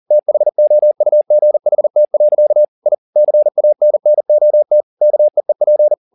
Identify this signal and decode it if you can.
Morse code, heard as a keyed tone, about 40 wpm, 600 Hz. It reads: THOUGHT. I CANNOT KEEP
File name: the old man and the sea 2 40wpm